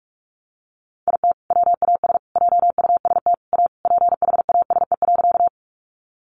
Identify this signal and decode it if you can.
Morse code, heard as a keyed tone, about 44 wpm, 700 Hz. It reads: IT WAS JUST A PHASE.
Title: Sentences from Top 1200 Words 45wpm